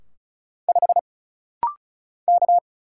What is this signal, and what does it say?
Morse code, heard as a keyed tone, about 34 wpm, 700 Hz. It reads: 5 K
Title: Single Letter-Number Alt - Rapid-Fire 35wpm